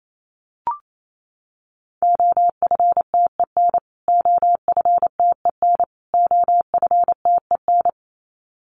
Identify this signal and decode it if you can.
Morse code, heard as a keyed tone, about 28 wpm, 700 Hz. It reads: OFTEN OFTEN OFTEN
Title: Top 500 Words Repeated 28wpm